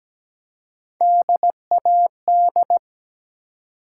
Morse code, heard as a keyed tone, about 17 wpm, 700 Hz. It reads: DAD
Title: Three Letter Words - 17wpm